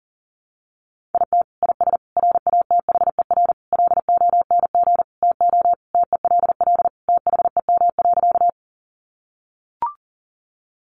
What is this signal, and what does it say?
Morse code, heard as a keyed tone, about 40 wpm, 700 Hz. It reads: IT IS RATHER LONG TO TELL THEM.
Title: Sentences from Top 700 Words 40wpm